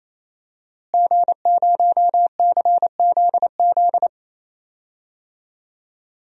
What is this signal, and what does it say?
Morse code, heard as a keyed tone, about 28 wpm, 690 Hz. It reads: G0CZZ
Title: DXCC-100 Call Signs - 5 Characters - Repeated 3x - 28wpm